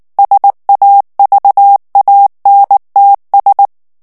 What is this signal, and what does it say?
Morse code, heard as a keyed tone, about 19 wpm, 800 Hz. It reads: SAVANTS
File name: CW / FICHIERS-MP3 / FRANCAIS TEXTE CLAIR 2 VITESSE 18 MOTS MINUTES